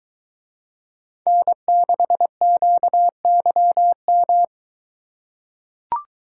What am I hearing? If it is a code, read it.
Morse code, heard as a keyed tone, about 22 wpm, 700 Hz. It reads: N6QYM